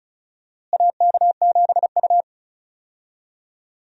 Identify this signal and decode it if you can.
Morse code, heard as a keyed tone, about 36 wpm, 700 Hz. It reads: AK7U
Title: USA Call Signs - 4 Characters - Repeated 2x - 35wpm